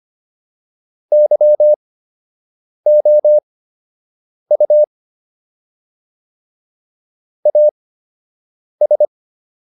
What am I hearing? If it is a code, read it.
Morse code, heard as a keyed tone, about 25 wpm, spaced as for 7 wpm, 600 Hz. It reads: YOU AS